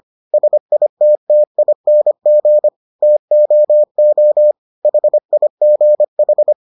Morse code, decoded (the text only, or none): SITTING TOO HIGH